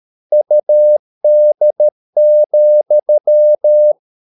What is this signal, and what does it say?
Morse code, heard as a keyed tone, about 13 wpm, 600 Hz. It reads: UD,
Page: Ferie 13 wpm